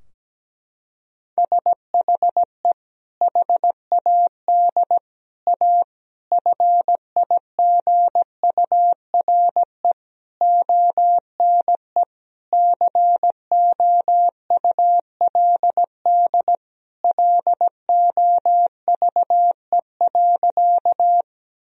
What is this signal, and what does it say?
Morse code, heard as a keyed tone, about 17 wpm, 700 Hz. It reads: SHE HAD A FIGURE ONE COULD LOVE.
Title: Sentences from Top 1000 Words - Review Part 2 - 17wpm